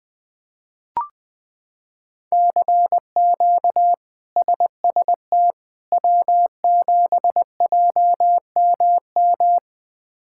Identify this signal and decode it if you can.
Morse code, heard as a keyed tone, about 20 wpm, 700 Hz. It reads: CQ SST W7JMM
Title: K1USN SST QSO Practice - 20wpm